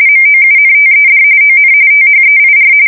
RTTY, about 45.45 baud. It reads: MOUNTED ON THEIR
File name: fsk170 / USS-Missouri